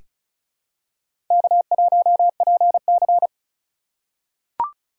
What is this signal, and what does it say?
Morse code, heard as a keyed tone, about 36 wpm, 700 Hz. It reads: K1PC
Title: USA Call Signs - 4x Character Spacing - 35wpm